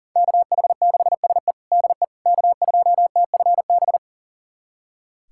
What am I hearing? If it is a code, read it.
Morse code, heard as a keyed tone, about 40 wpm, 700 Hz. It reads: KH6SE DE K2TFB
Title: USA Callsign DE Callsign - 5 Characters - 40wpm